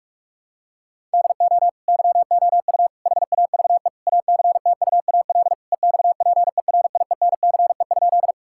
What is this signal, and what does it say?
Morse code, heard as a keyed tone, about 43 wpm, 700 Hz. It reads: DO YOU HAVE ACTUAL EXPERIENCE?
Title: Sentences from Top 1000 Words - Review Part 5 - 45wpm